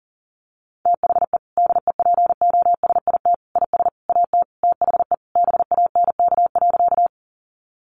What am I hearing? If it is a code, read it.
Morse code, heard as a keyed tone, about 40 wpm, 700 Hz. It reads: THE DEPOSIT IS AT THE BANK.
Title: Sentences from Top 1300 Words 40wpm